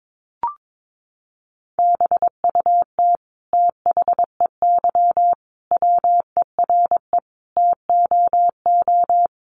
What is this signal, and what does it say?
Morse code, heard as a keyed tone, about 22 wpm, 700 Hz. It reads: BUT THEY WERE TOO
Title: Sentences from Top 1500 Words 22wpm